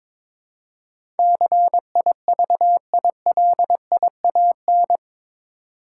Morse code, heard as a keyed tone, about 22 wpm, 700 Hz. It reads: CIVILIAN
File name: practice / 22wpm / Eight Letter Words - 22wpm